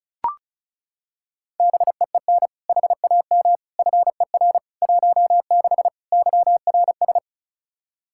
Morse code, heard as a keyed tone, about 36 wpm, 700 Hz. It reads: BEEN HAM FER 16 YRS
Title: Practice QSO - Exchange 3 - More Info - 35wpm